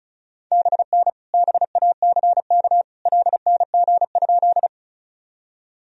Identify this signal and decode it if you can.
Morse code, heard as a keyed tone, about 34 wpm, 690 Hz. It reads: BN BACK LNG?
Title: Sentences from Top 200 Words and Abbreviations 35wpm